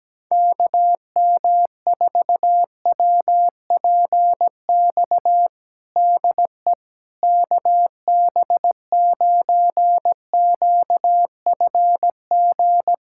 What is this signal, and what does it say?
Morse code, heard as a keyed tone, about 17 wpm, 700 Hz. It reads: KM4WPX DE KB9QFG
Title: USA Callsign DE Callsign - 6 Characters - 17wpm